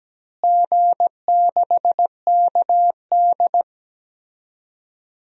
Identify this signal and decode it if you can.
Morse code, heard as a keyed tone, about 17 wpm, 700 Hz. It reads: G6KD
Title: DXCC-100 Call Signs - 4 Characters - Repeated 3x - 17wpm